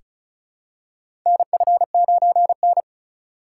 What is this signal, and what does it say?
Morse code, heard as a keyed tone, about 36 wpm, 700 Hz. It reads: NF9N
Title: USA Call Signs - 2x Character Spacing - 35wpm